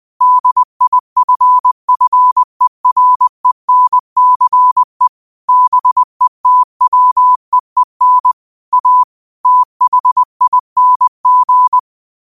Morse code, heard as a keyed tone, about 20 wpm, 1000 Hz. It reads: DIFFERENCE BETWEEN A THING